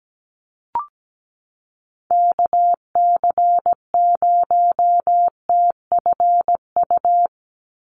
Morse code, heard as a keyed tone, about 17 wpm, 700 Hz. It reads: KC0TFU
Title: USA Call Signs 17wpm